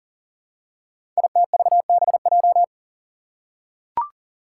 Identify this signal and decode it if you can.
Morse code, heard as a keyed tone, about 43 wpm, 700 Hz. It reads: ITVBJ
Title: Code Groups - 5 Letters - 40wpm